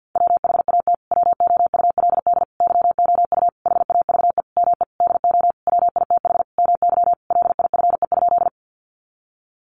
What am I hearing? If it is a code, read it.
Morse code, heard as a keyed tone, about 50 wpm, 700 Hz. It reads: WHAT WOULD YOU HAVE ME DO WITH MY LIFE?
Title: Sentences from Top 300 Words 50wpm